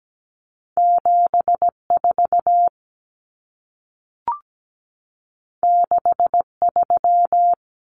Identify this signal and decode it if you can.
Morse code, heard as a keyed tone, about 17 wpm, 700 Hz. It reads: 74 63
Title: Common Cut Numbers - 2-Digit 17wpm